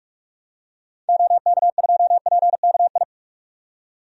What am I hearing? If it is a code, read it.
Morse code, heard as a keyed tone, about 46 wpm, 700 Hz. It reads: OK2PKI
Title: DXCC-100 Call Signs - 6 Characters - 45wpm